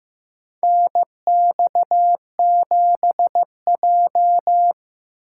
Morse code, heard as a keyed tone, about 15 wpm, 700 Hz. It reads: NX7J
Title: USA Call Signs - 8x Character Spacing - 15wpm